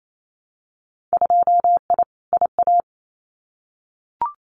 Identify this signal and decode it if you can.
Morse code, heard as a keyed tone, about 28 wpm, 700 Hz. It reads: 2I IA
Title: Winter Field Day Exchanges Encore 28wpm